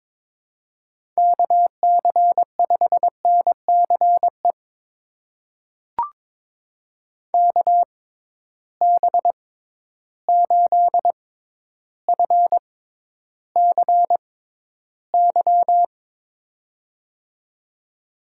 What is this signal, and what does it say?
Morse code, heard as a keyed tone, about 22 wpm, spaced as for 11 wpm, 700 Hz. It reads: KC5NCE KB8FCY
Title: USA Call Signs - 6 Characters - 6x Character Spacing - 22wpm